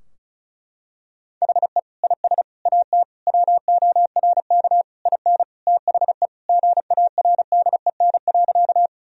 Morse code, heard as a keyed tone, about 34 wpm, 700 Hz. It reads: HE IS AT WORK IN THE GARDEN.
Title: Sentences from Top 1000 Words - Review Part 4 - 35wpm